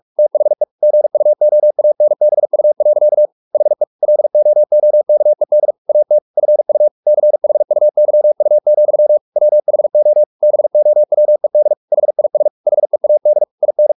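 Morse code, heard as a keyed tone, about 44 wpm, 600 Hz. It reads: THE GUOANBU. HE LOOKED AT FU CHUYU, WHO BOWED HIS HEAD IN
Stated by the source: Morse practice at 45 wpm, the novel Never by Ken Follett in English, from CW med Gnister.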